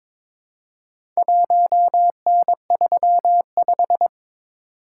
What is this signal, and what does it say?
Morse code, heard as a keyed tone, about 22 wpm, 700 Hz. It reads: 1N35
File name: practice / 22wpm / Four Letter-Number Combinations - 22wpm